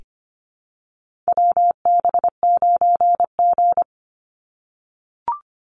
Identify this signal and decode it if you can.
Morse code, heard as a keyed tone, about 25 wpm, 700 Hz. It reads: WB9G
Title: Call Signs - 25wpm